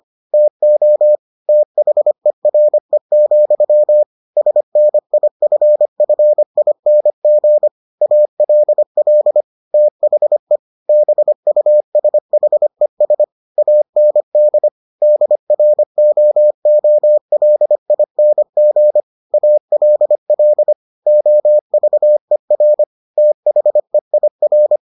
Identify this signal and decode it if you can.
Morse code, heard as a keyed tone, about 25 wpm, 600 Hz. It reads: TO THERE, SNIFFING ALL THE BUSHES AND DROOLING ALL OVER THEIR